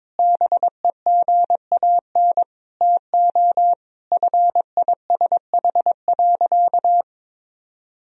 Morse code, heard as a keyed tone, about 22 wpm, 700 Hz. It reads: BEGAN TO FISH.
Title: Sentences from Top 300 Words and Abbreviations 22wpm